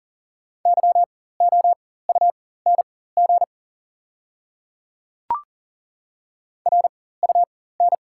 Morse code, heard as a keyed tone, about 40 wpm, spaced as for 19 wpm, 690 Hz. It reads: YOUNG RUN
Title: 4x Character Spacing - Top 500 Words - 40wpm